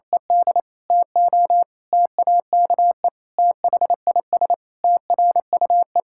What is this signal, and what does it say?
Morse code, heard as a keyed tone, about 28 wpm, 700 Hz. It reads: ED TO TAKE THIS TRUE
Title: Pangram Challenge v2 - 28wpm